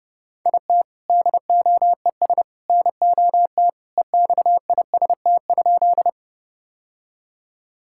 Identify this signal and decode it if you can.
Morse code, heard as a keyed tone, about 30 wpm, 700 Hz. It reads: IT DOES NOT EXIST?
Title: Sentences from Top 1000 Words - Review Part 5 - 30wpm